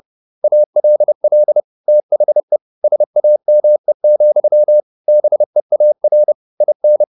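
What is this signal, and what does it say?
Morse code, heard as a keyed tone, about 30 wpm, 600 Hz. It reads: ALL THE SAME, BEAR IN
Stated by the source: Morse code at 30 wpm, the novel Never by Ken Follett in English, from CW med Gnister.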